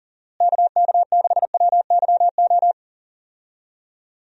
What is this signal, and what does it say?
Morse code, heard as a keyed tone, about 40 wpm, 700 Hz. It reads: KK6WYO
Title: USA Call Signs - 6 Characters - Repeated 2x - 40wpm